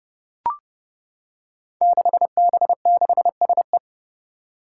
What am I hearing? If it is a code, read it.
Morse code, heard as a keyed tone, about 30 wpm, 700 Hz. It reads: 6B6SE
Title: Five Challenging Letter-Number Combinations 30wpm